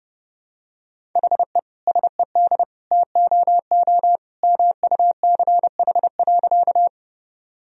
Morse code, heard as a keyed tone, about 30 wpm, 700 Hz. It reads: HE SED TOO MUCH.
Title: Sentences from Top 200 Words and Abbreviations 30wpm